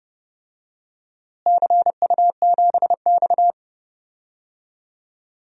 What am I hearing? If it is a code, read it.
Morse code, heard as a keyed tone, about 30 wpm, 700 Hz. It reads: CU7X
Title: DXCC-100 Call Signs - 4 Characters - Repeated 2x - 30wpm